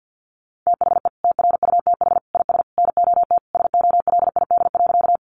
Morse code, heard as a keyed tone, about 51 wpm, 700 Hz. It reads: THE TRUTH IS NOT SOLID.
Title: Sentences from Top 1500 Words 50wpm